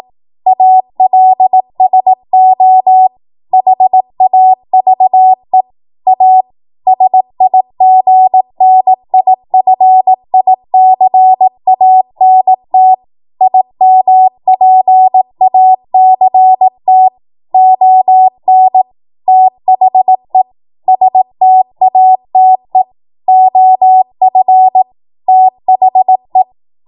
Morse code, heard as a keyed tone, about 18 wpm, 750 Hz. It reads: ALSO HAVE A SIGNIFICANT IMPACT ON THE STATE OF THE
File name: Morse / 18WPM